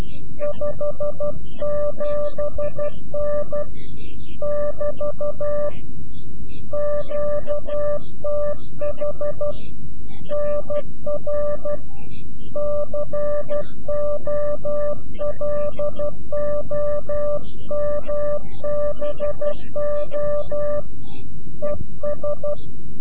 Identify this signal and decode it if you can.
Morse code, heard as a keyed tone, about 12 wpm, 590 Hz. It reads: 57N <BT> QTH NR COLOMBO ES
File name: CW-QSO / CQ Call / CQ-QSO